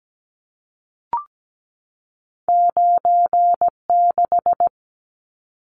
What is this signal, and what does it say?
Morse code, heard as a keyed tone, about 17 wpm, 700 Hz. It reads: 96
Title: Numbers - 2-Digit 17wpm